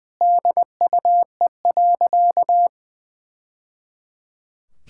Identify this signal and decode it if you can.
Morse code, heard as a keyed tone, about 20 wpm, 700 Hz. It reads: DUE.
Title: Sentences from Top 1000 Words - Review Part 6 - 20wpm